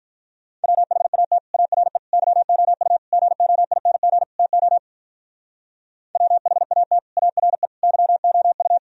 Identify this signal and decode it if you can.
Morse code, heard as a keyed tone, about 43 wpm, 700 Hz. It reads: WHAT ARE YOU GOING TO WHAT ARE YOU
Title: Speed-Racing - Sets of 5 Words 35wpm